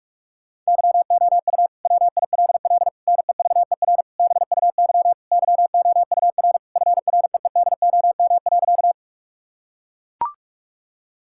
Morse code, heard as a keyed tone, about 41 wpm, 700 Hz. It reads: YOU WILL NEVER BUY YOUR FREEDOM.
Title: Sentences from Top 1000 Words - Review Part 5 - 45wpm